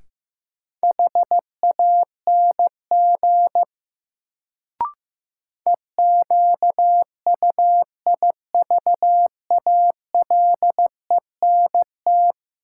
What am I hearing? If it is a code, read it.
Morse code, heard as a keyed tone, about 15 wpm, 700 Hz. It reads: HANG EQUIVALENT
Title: Sentences from Top 1600 Words - New Word Drill - 15wpm